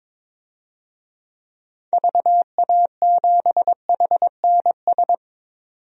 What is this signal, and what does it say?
Morse code, heard as a keyed tone, about 22 wpm, 700 Hz. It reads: VA7HNS
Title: DXCC-100 Call Signs - 6 Characters - 22wpm